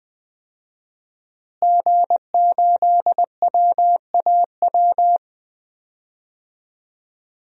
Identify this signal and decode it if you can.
Morse code, heard as a keyed tone, about 20 wpm, 700 Hz. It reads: G8WAW
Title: DXCC-100 Call Signs - 5 Characters - Repeated 3x - 20wpm